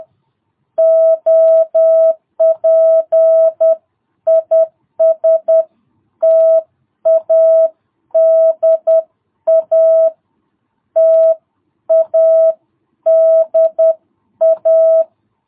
Morse code, heard as a keyed tone, about 10 wpm, 660 Hz. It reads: OP IS TADA TADA
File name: jh8cbh / 2022cw